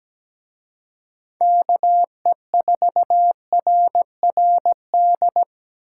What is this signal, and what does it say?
Morse code, heard as a keyed tone, about 20 wpm, 700 Hz. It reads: KE4RRD
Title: Speed Racing - USA 2x3 Call Signs 17wpm